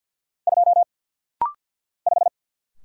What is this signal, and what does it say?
Morse code, heard as a keyed tone, about 50 wpm, 690 Hz. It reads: 2 5
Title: Single Letter-Number - Rapid-Fire 50wpm